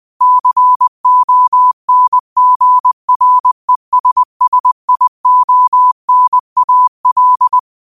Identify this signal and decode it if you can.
Morse code, heard as a keyed tone, about 20 wpm, 1000 Hz. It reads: CONGRESSIONAL
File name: qotdfiles / 20wpm